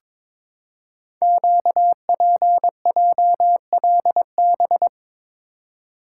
Morse code, heard as a keyed tone, about 22 wpm, 700 Hz. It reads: QPJLB
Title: Code Groups - 5 Letters - 22wpm